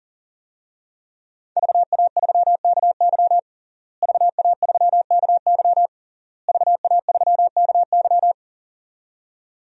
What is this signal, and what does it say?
Morse code, heard as a keyed tone, about 40 wpm, 700 Hz. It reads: VA3KY VA3KY VA3KY
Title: DXCC-100 Call Signs - 5 Characters - Repeated 3x - 40wpm